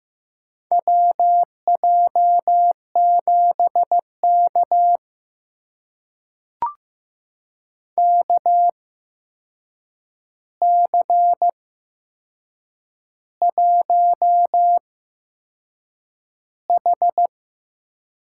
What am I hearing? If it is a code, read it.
Morse code, heard as a keyed tone, about 15 wpm, spaced as for 6 wpm, 700 Hz. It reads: WJ7K KC1H
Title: USA Call Signs - 4 Characters - 8x Character Spacing - 15wpm